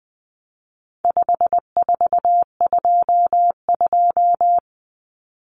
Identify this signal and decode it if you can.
Morse code, heard as a keyed tone, about 20 wpm, 700 Hz. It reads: 5422
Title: Numbers - 4-Digit 20wpm